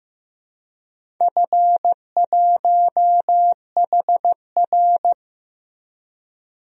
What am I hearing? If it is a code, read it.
Morse code, heard as a keyed tone, about 16 wpm, 700 Hz. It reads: F1HR
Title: DXCC-100 Call Signs - 4 Characters - Repeated 3x - 15wpm